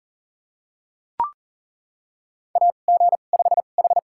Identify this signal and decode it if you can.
Morse code, heard as a keyed tone, about 37 wpm, spaced as for 28 wpm, 700 Hz. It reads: AG5H
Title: USA Call Signs - 4 Characters - 2x Character Spacing - 40wpm